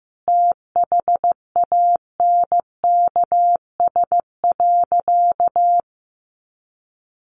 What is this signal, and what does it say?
Morse code, heard as a keyed tone, about 15 wpm, 700 Hz. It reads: THANKS.
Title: Sentences from Top 700 Words 15wpm